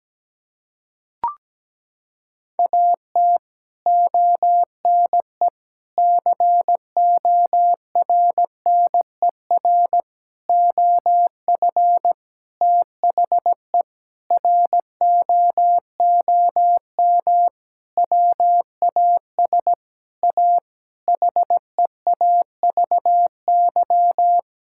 Morse code, heard as a keyed tone, about 17 wpm, 700 Hz. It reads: AT ONE CORNER OF THE ROOM WAS A HEAVY
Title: Sentences from Top 1600 Words 17wpm